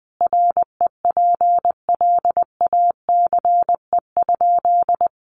Morse code, heard as a keyed tone, about 20 wpm, 700 Hz. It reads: REPLACE?
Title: Sentences from Top 1400 Words 20wpm